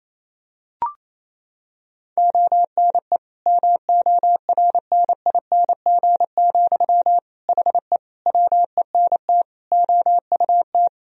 Morse code, heard as a keyed tone, about 27 wpm, 700 Hz. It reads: ONE MORNING, HE WENT OUT
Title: Sentences from Top 1000 Words - Review Part 2 - 28wpm